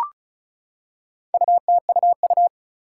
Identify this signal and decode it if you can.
Morse code, heard as a keyed tone, about 36 wpm, 700 Hz. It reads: UTUU
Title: Four Challenging Letter-Number Combinations 35wpm